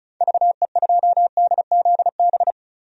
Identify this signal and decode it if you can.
Morse code, heard as a keyed tone, about 36 wpm, 700 Hz. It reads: VE2DZB
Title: DXCC-100 Call Signs - 6 Characters - 35wpm